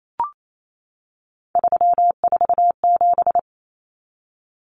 Morse code, heard as a keyed tone, about 28 wpm, 700 Hz. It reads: 347
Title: Numbers - 3-Digit 28wpm